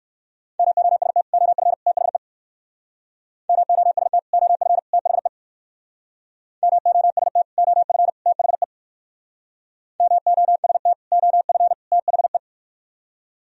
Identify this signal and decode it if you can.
Morse code, heard as a keyed tone, about 54 wpm, 700 Hz. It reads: MOST OF THE MOST OF THE MOST OF THE MOST OF THE
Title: Speed-Racing - Sets of 3 Words 45wpm